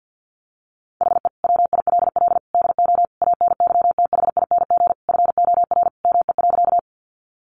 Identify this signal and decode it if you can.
Morse code, heard as a keyed tone, about 50 wpm, 700 Hz. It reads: HE WILL DO ANYTHING FOR ME.
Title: Sentences from Top 1000 Words 50wpm